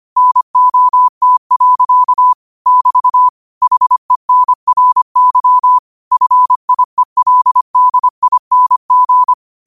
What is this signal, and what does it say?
Morse code, heard as a keyed tone, about 25 wpm, 1000 Hz. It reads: NOT. <BT> HENRY FIELDING
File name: 25wpm